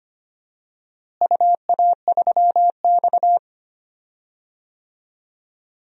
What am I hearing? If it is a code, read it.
Morse code, heard as a keyed tone, about 25 wpm, 700 Hz. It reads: UA3X
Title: DXCC-100 Call Signs - 4 Characters - Repeated 2x - 25wpm